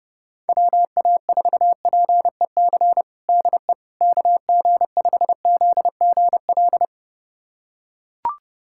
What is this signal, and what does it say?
Morse code, heard as a keyed tone, about 28 wpm, 700 Hz. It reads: WA4PEC DE KG5ZGL
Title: USA Callsign DE Callsign - 6 Characters - 30wpm